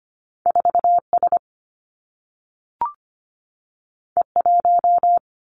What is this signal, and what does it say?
Morse code, heard as a keyed tone, about 25 wpm, 700 Hz. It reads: 4S E1
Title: Two Challenging Letter-Number Combinations 25wpm